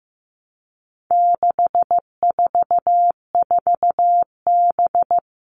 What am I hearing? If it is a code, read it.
Morse code, heard as a keyed tone, about 15 wpm, 700 Hz. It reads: 644B
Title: All Cut Numbers - 4-Digit 15wpm